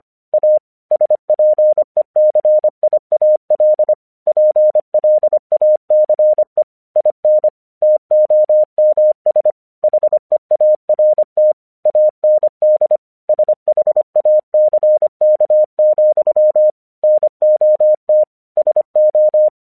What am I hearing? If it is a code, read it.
Morse code, heard as a keyed tone, about 25 wpm, 610 Hz. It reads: A SPECIAL PLACE IN TOMS HEART AND SHACK, NOT SO
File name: 1 English 25wpm / AQC12